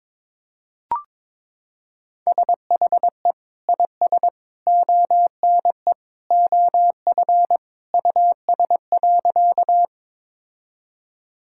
Morse code, heard as a keyed tone, about 22 wpm, 700 Hz. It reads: SHE IS ONE OF US.
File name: practice / 22wpm / Sentences with 3-Word Phrases - 22wpm